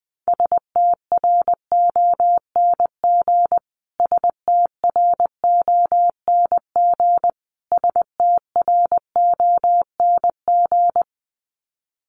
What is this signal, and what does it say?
Morse code, heard as a keyed tone, about 20 wpm, 700 Hz. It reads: STRONG STRONG STRONG
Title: Top 500 Words Repeated 20wpm